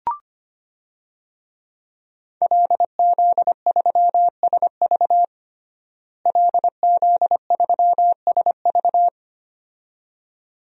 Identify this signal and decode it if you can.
Morse code, heard as a keyed tone, about 25 wpm, 700 Hz. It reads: LZ3SV LZ3SV
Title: DXCC-100 Call Signs - 5 Characters - Repeated 2x - 25wpm